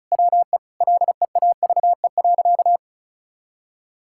Morse code, heard as a keyed tone, about 34 wpm, 700 Hz. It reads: WE LEAVE.
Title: Sentences from Top 1000 Words - Review Part 5 - 35wpm